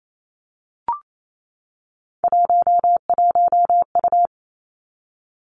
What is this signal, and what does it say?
Morse code, heard as a keyed tone, about 28 wpm, 700 Hz. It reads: 11U
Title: Three Challenging Letter-Number Combinations 28wpm